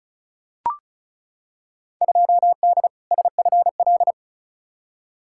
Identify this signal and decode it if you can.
Morse code, heard as a keyed tone, about 36 wpm, 700 Hz. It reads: 2D SFL
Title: Field Day Exchanges Encore - 35wpm